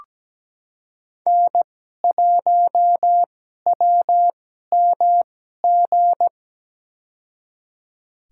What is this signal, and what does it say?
Morse code, heard as a keyed tone, about 17 wpm, spaced as for 12 wpm, 700 Hz. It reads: N1WMG
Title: USA Call Signs - 5 Characters - 2x Character Spacing - 17wpm